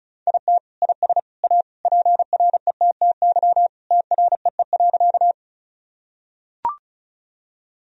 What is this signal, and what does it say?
Morse code, heard as a keyed tone, about 36 wpm, 700 Hz. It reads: IT IS A PRETTY TREE.
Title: Sentences from Top 1000 Words - Review Part 1 - 35wpm